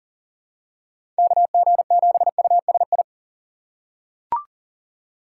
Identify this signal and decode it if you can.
Morse code, heard as a keyed tone, about 37 wpm, 700 Hz. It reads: KG7USI
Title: USA Call Signs - 6 Characters - 8x Character Spacing - 40wpm